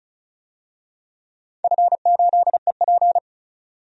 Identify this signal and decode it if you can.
Morse code, heard as a keyed tone, about 36 wpm, 700 Hz. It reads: F8EP
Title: DXCC-100 Call Signs - 4 Characters - 35wpm